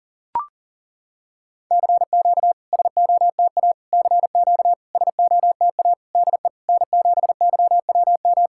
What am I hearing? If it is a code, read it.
Morse code, heard as a keyed tone, about 39 wpm, 700 Hz. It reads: CQ SOTA CQ SOTA DE N7YWM
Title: SOTA QSO Practice 40wpm